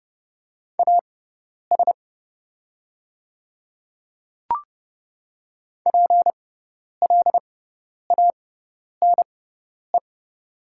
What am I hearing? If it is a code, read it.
Morse code, heard as a keyed tone, about 30 wpm, spaced as for 10 wpm, 700 Hz. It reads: AS PLANE
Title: 6x Character Spacing - Top 500 Words - 30wpm